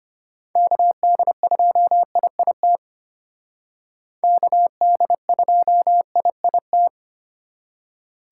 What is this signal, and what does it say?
Morse code, heard as a keyed tone, about 33 wpm, 700 Hz. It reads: KD2IIT KD2IIT